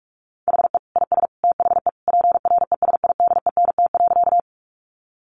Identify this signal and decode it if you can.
Morse code, heard as a keyed tone, about 44 wpm, 700 Hz. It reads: HE IS THE PRESIDENT.
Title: Sentences from Top 1000 Words 45wpm